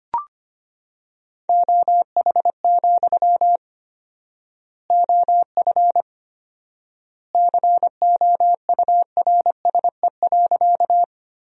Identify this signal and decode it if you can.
Morse code, heard as a keyed tone, about 25 wpm, 700 Hz. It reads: OH, OF COURSE.
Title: 4x Word-Spacing - Sentences from Top 500 Words - 25wpm